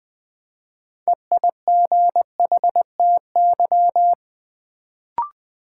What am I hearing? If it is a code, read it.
Morse code, heard as a keyed tone, about 21 wpm, 700 Hz. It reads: EIGHTY
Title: Build A Word - 2 Letters At A Time - 6-Letter Words - 20wpm